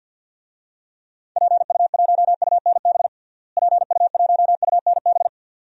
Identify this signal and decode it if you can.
Morse code, heard as a keyed tone, about 52 wpm, 700 Hz. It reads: PU1UNB PU1UNB
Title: DXCC-100 Call Signs - 6 Characters - Repeated 2x - 50wpm